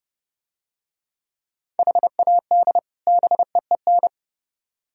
Morse code, heard as a keyed tone, about 31 wpm, 700 Hz. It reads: HAD BEEN